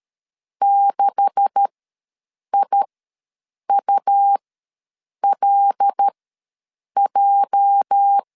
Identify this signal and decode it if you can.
Morse code, heard as a keyed tone, about 13 wpm, spaced as for 7 wpm, 800 Hz. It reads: 6IULJ
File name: LEZIONI / 9 286 ws 12wpm 3351